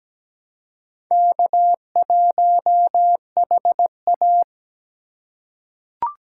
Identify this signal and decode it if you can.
Morse code, heard as a keyed tone, about 17 wpm, 700 Hz. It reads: K1HA
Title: USA Call Signs - 4 Characters - 6x Character Spacing - 17wpm